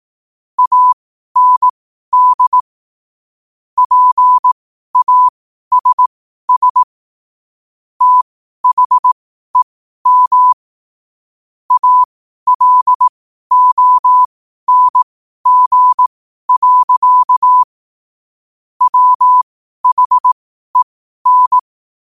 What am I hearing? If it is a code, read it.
Morse code, heard as a keyed tone, about 18 wpm, spaced as for 12 wpm, 1000 Hz. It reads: AND PASS THEM ALONG. WHEN
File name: qotdfiles / 12wpm